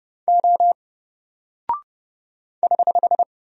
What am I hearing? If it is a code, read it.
Morse code, heard as a keyed tone, about 30 wpm, 700 Hz. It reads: O <HH>
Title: Single Character (Letter:Number:Punctuation:Prosign) 30wpm